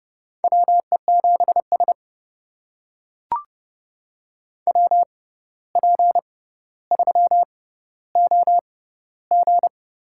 Morse code, heard as a keyed tone, about 32 wpm, spaced as for 15 wpm, 700 Hz. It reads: WE7S WP3OG
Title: USA Call Signs - 6x Character Spacing - 30wpm